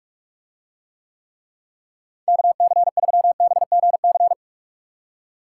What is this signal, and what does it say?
Morse code, heard as a keyed tone, about 45 wpm, 700 Hz. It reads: KC3BGC